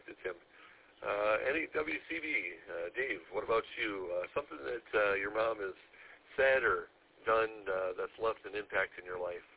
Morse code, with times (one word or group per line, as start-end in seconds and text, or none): none